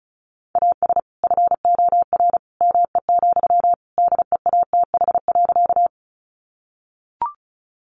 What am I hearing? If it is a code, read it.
Morse code, heard as a keyed tone, about 34 wpm, 700 Hz. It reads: AS FOR ME, DEATH.
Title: Sentences from Top 1100 Words 35wpm